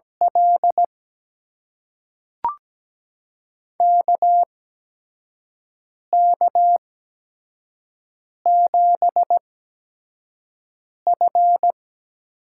Morse code, heard as a keyed tone, about 17 wpm, spaced as for 5 wpm, 700 Hz. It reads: L KK7F
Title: USA Call Signs - 5 Characters - 8x Character Spacing - 17wpm